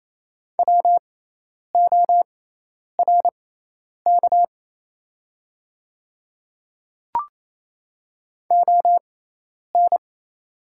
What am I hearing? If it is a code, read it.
Morse code, heard as a keyed tone, about 28 wpm, spaced as for 10 wpm, 700 Hz. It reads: WORK ON
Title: 6x Character Spacing - Top 300 Words - 28wpm